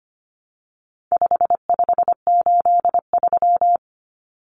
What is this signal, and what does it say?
Morse code, heard as a keyed tone, about 25 wpm, 700 Hz. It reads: 5583
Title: Common Cut Numbers - 4-Digit 25wpm